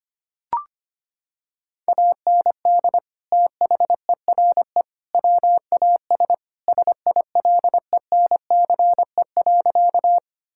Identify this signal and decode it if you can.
Morse code, heard as a keyed tone, about 25 wpm, 700 Hz. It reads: AND THERE WAS SILENCE.
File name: practice / 25wpm / Sentences with 3-Word Phrases - 25wpm